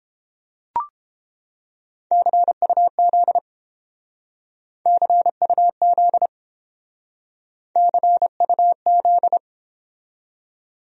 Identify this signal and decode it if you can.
Morse code, heard as a keyed tone, about 29 wpm, 700 Hz. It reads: CUZ CUZ CUZ